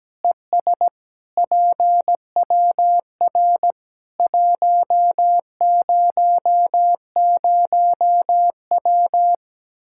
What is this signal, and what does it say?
Morse code, heard as a keyed tone, about 17 wpm, 700 Hz. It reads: ES PWR 100W
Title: Practice QSO - Exchange 2 - Info - 17wpm